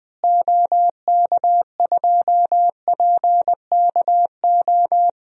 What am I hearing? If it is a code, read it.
Morse code, heard as a keyed tone, about 20 wpm, 700 Hz. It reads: OK2PKO
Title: DXCC-100 Call Signs - 6 Characters - Repeated 2x - 20wpm